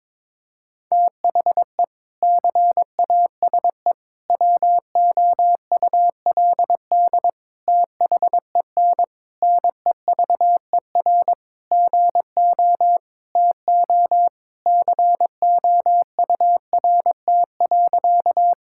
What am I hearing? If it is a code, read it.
Morse code, heard as a keyed tone, about 22 wpm, 700 Hz. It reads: THE CASE WOULD THEN NEVER GO TO COURT.
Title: Sentences from Top 1600 Words 22wpm